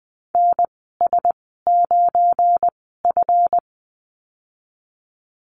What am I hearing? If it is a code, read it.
Morse code, heard as a keyed tone, about 20 wpm, spaced as for 15 wpm, 700 Hz. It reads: NS9F
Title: 186 Lesson 30g - Letter b - Call Signs 20wpm